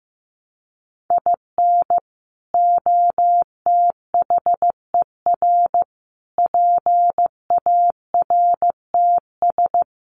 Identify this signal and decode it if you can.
Morse code, heard as a keyed tone, about 15 wpm, 700 Hz. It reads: IN OTHER PARTS
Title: Sets of 5 Words 15wpm